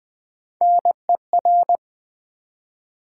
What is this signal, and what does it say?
Morse code, heard as a keyed tone, about 20 wpm, 700 Hz. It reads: NER
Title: Build A Word - 1 Letter At A Time - 3 to 5-Letter Words - 20wpm